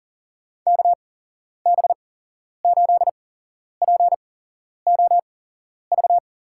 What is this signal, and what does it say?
Morse code, heard as a keyed tone, about 39 wpm, spaced as for 11 wpm, 700 Hz. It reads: KB8POV
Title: USA Call Signs - 6 Characters - 8x Character Spacing - 40wpm